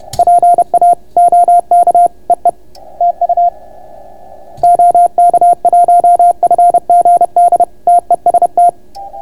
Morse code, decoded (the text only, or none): PA OK EE TU OK1FGD TEST